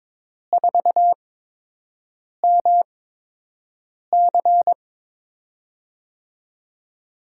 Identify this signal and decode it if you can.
Morse code, heard as a keyed tone, about 23 wpm, spaced as for 6 wpm, 700 Hz. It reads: 4MC
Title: USA Call Signs - 5 Characters - 8x Character Spacing - 22wpm